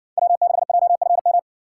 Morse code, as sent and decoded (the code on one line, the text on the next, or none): .-- -.... .--- ..-. --
W6JFM